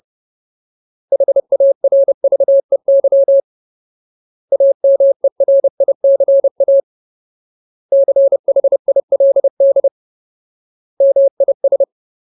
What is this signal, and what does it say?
Morse code, heard as a keyed tone, about 30 wpm, 550 Hz. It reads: HARVEY AMERICA CHILD MIS
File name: Ultimate Word List / 30WPM Ultimate